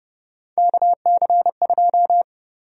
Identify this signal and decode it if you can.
Morse code, heard as a keyed tone, about 30 wpm, 700 Hz. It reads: KC2
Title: USA Call Sign Prefixes - 30wpm